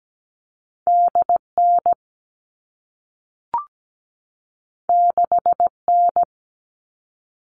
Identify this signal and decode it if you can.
Morse code, heard as a keyed tone, about 17 wpm, 700 Hz. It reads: DN 6N
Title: All Cut Numbers - 2-Digit 17wpm